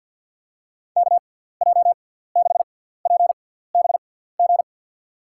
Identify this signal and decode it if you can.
Morse code, heard as a keyed tone, about 52 wpm, spaced as for 17 wpm, 700 Hz. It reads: KJ6PBG
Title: USA Call Signs - 6 Characters - 6x Character Spacing - 50wpm